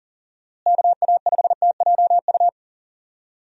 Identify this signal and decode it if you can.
Morse code, heard as a keyed tone, about 39 wpm, 700 Hz. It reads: KA5TJU
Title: USA Call Signs - 6 Characters - Repeated 2x - 40wpm